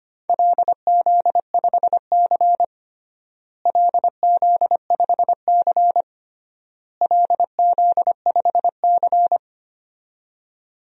Morse code, heard as a keyed tone, about 25 wpm, 700 Hz. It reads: LZ5C LZ5C LZ5C
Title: DXCC-100 Call Signs - 4 Characters - Repeated 3x - 25wpm